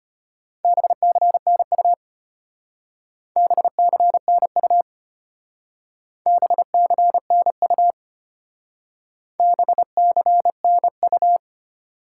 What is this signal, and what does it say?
Morse code, heard as a keyed tone, about 32 wpm, 700 Hz. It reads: BCNU BCNU BCNU BCNU